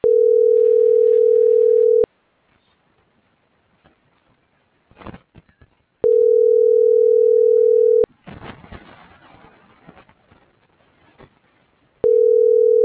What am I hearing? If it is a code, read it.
Ringback tone (North America), 440 + 480 Hz.
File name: CENTRAL MULTIMODAL / out josimar.cruz - .76496